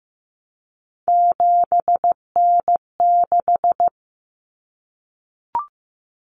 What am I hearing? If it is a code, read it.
Morse code, heard as a keyed tone, about 15 wpm, 700 Hz. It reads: 7N6
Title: Common Cut Numbers - 3-Digit 15wpm